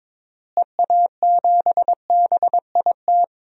Morse code, heard as a keyed tone, about 22 wpm, 700 Hz. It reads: EA7BIT
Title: DXCC-100 Call Signs - 6 Characters - Repeated 2x - 22wpm